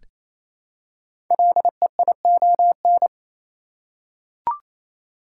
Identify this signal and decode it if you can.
Morse code, heard as a keyed tone, about 29 wpm, 700 Hz. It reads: LEION
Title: Code Groups - 5 Letter:Numbers - 28wpm